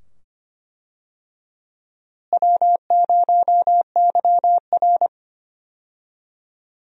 Morse code, heard as a keyed tone, about 25 wpm, 700 Hz. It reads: W0YR
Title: USA Call Signs - 4 Characters - Repeated 2x - 25wpm